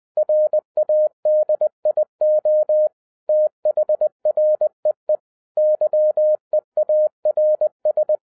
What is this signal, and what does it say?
Morse code, heard as a keyed tone, about 20 wpm, 610 Hz. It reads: RADIO THREE YEARS
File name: 2012-01-mp3 / n05-CAME BACK TO HAM RADIO AFTER 30 YEARS OF INTERVAL-20wpm